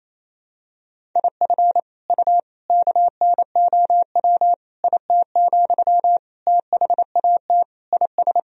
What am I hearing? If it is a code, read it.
Morse code, heard as a keyed tone, about 28 wpm, 700 Hz. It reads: IF U KNOW IT, THAT IS
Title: Sentences from Top 400 Words and Abbreviations 28wpm